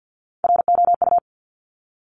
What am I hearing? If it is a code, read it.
Morse code, heard as a keyed tone, about 50 wpm, 700 Hz. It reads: ROU
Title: Top Three Letter Combinations in Words 50wpm